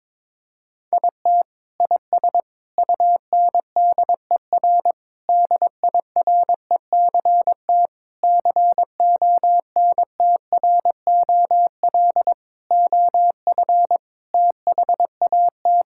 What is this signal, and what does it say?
Morse code, heard as a keyed tone, about 21 wpm, 700 Hz. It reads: IT IS UNDER DIRECT CONTROL OF THAT
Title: Sentences from Top 1000 Words - Review Part 5 - 22wpm